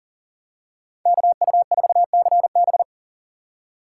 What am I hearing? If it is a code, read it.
Morse code, heard as a keyed tone, about 40 wpm, 700 Hz. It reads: KU4CB
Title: USA Call Signs - 5 Characters - 2x Character Spacing - 40wpm